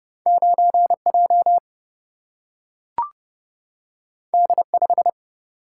Morse code, heard as a keyed tone, about 30 wpm, 700 Hz. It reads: 9J D5
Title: Single-Letter Single-Number 30wpm